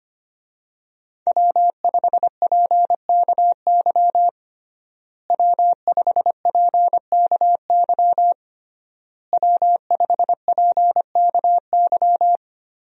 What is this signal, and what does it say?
Morse code, heard as a keyed tone, about 25 wpm, 700 Hz. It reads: W5PKY W5PKY W5PKY
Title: USA Call Signs - 5 Characters - Repeated 3x - 25wpm